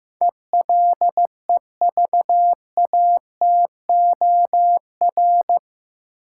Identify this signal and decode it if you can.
Morse code, heard as a keyed tone, about 15 wpm, 700 Hz. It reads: ELEVATOR
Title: Sentences from Top 1600 Words - New Word Drill - 15wpm